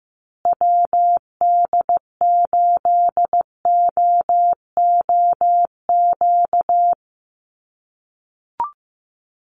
Morse code, heard as a keyed tone, about 15 wpm, 700 Hz. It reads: WD8OOQ
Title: USA Call Signs Repeated 2x 15wpm